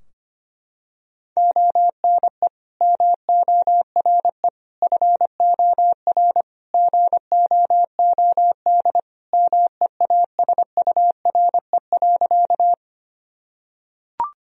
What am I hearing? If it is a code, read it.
Morse code, heard as a keyed tone, about 25 wpm, 700 Hz. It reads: ONE MORE FOR GOOD MEASURE.
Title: Sentences from Top 1000 Words - Review Part 2 - 25wpm